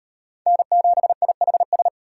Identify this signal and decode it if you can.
Morse code, heard as a keyed tone, about 37 wpm, 700 Hz. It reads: N7IHS